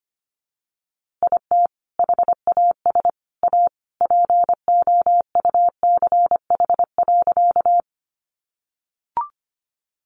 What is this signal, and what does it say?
Morse code, heard as a keyed tone, about 25 wpm, 700 Hz. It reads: IT HAS A POUCH.
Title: Easy Phrases 25wpm